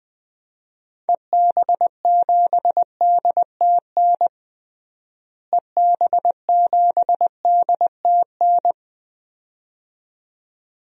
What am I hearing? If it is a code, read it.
Morse code, heard as a keyed tone, about 21 wpm, 700 Hz. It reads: EB7DTN EB7DTN
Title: DXCC-100 Call Signs - 6 Characters - Repeated 2x - 20wpm